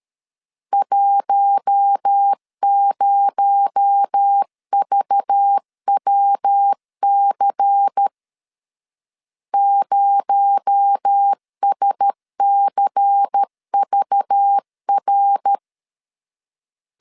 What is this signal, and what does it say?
Morse code, heard as a keyed tone, about 13 wpm, 800 Hz. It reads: 10VWC 0SCVR
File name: CW / DET / 1 6 1500C S B 12wpm 3431